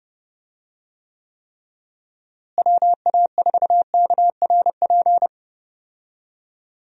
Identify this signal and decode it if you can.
Morse code, heard as a keyed tone, about 30 wpm, 700 Hz. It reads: WA4KRP
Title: USA Call Signs - 6 Characters - Repeated 3x - 30wpm